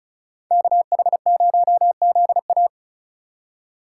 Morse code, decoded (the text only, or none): KH0ZA